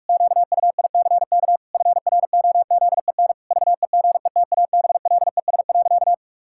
Morse code, heard as a keyed tone, about 41 wpm, 700 Hz. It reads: QUICK FROZEN VEGETABLES.
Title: Pangram Challenge v2 - 45wpm